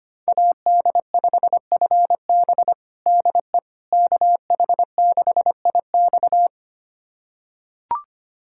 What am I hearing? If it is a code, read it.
Morse code, heard as a keyed tone, about 25 wpm, 700 Hz. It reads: AD5FB DE KH6IX
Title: USA Callsign DE Callsign - 5 Characters - 25wpm